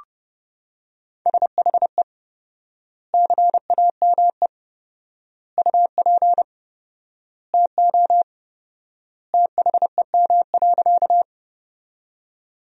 Morse code, heard as a keyed tone, about 30 wpm, 700 Hz. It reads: SHE CAME UP TO THEM.
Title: 4x Word-Spacing - Sentences from Top 200 Words - 30wpm